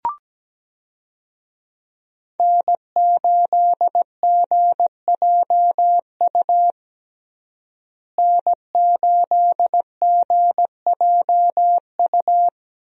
Morse code, heard as a keyed tone, about 17 wpm, 700 Hz. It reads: N8GJU N8GJU
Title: USA Call Signs - 5 Characters - Repeated 2x - 17wpm